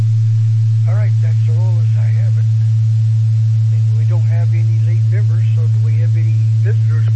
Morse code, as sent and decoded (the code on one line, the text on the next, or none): none